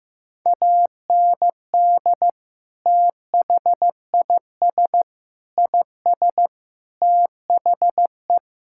Morse code, heard as a keyed tone, about 15 wpm, 700 Hz. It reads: AND THIS IS THE
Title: Sentences from Top 1000 Words - Review Part 5 - 15wpm